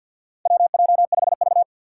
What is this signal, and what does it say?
Morse code, heard as a keyed tone, about 50 wpm, 700 Hz. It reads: WJ5V
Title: Four Challenging Letter-Number Combinations 50wpm